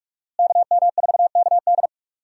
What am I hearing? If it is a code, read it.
Morse code, heard as a keyed tone, about 46 wpm, 700 Hz. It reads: KM4KD
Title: USA Call Signs - 5 Characters - Repeated 3x - 45wpm